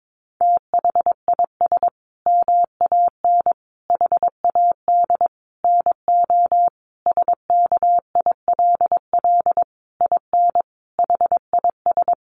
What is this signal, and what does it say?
Morse code, heard as a keyed tone, about 22 wpm, 700 Hz. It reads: THIS MAN HAD NO SKILL IN HIS
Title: Sentences from Top 1200 Words 22wpm